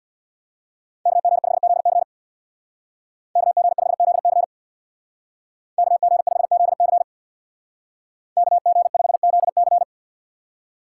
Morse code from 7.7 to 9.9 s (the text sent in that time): KG5ZC